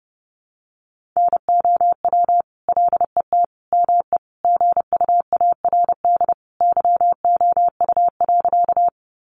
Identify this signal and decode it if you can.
Morse code, heard as a keyed tone, about 30 wpm, 700 Hz. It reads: NOW LET ME GUARD YOU.
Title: Sentences from Top 700 Words 30wpm